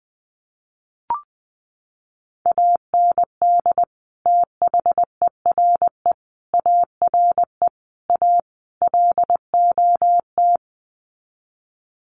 Morse code, heard as a keyed tone, about 20 wpm, 700 Hz. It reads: AND THERE ARE A LOT
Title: Sets of 5 Words Encore2 20wpm